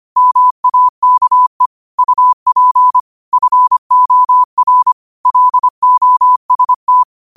Morse code, heard as a keyed tone, about 25 wpm, 1000 Hz. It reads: MAKE UP FOR LOST
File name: qotdfiles / 25wpm